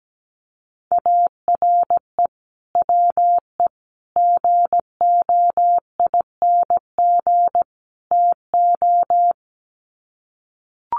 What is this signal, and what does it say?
Morse code, heard as a keyed tone, about 17 wpm, 700 Hz. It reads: ARE WE GOING TO
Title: Sets of 4 Words 17wpm